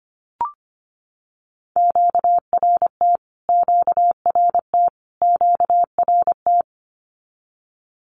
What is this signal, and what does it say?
Morse code, heard as a keyed tone, about 25 wpm, 700 Hz. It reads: QRT QRT QRT
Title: Q-Codes Repeated 3x 25wpm